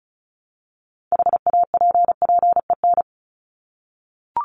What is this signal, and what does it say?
Morse code, heard as a keyed tone, about 34 wpm, 700 Hz. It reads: HAPPEN
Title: ED Suffix - Top 100 Words with Base Words 35wpm